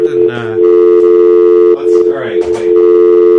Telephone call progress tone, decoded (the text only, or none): dial tone